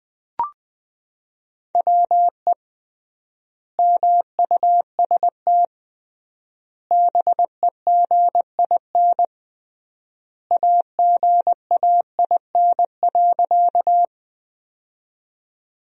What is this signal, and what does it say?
Morse code, heard as a keyed tone, about 20 wpm, 700 Hz. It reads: WE MUST BEGIN AGAIN.
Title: 3x Word-Spacing - Sentences from Top 300 Words - 20wpm